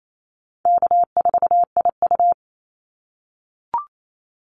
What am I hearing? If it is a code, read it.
Morse code, heard as a keyed tone, about 28 wpm, 700 Hz. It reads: K4IU
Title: Easy Call Signs - 28wpm